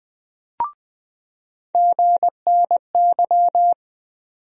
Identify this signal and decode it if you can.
Morse code, heard as a keyed tone, about 20 wpm, 700 Hz. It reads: GNY
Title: USA Call Sign Postfixes - 3 Letters 20wpm